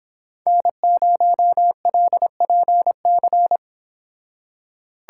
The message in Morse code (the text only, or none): N0LPC